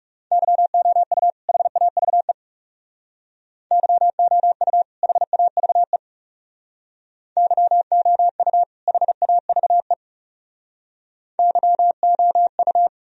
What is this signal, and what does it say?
Morse code, heard as a keyed tone, about 36 wpm, 700 Hz. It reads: YOU HAVE YOU HAVE YOU HAVE YOU
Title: Speed-Racing - Sets of 2 Words 30wpm